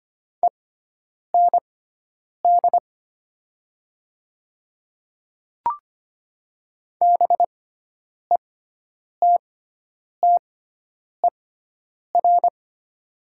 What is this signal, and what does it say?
Morse code, heard as a keyed tone, about 25 wpm, spaced as for 9 wpm, 700 Hz. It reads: END BETTER
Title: 6x Character Spacing - Top 500 Words - 25wpm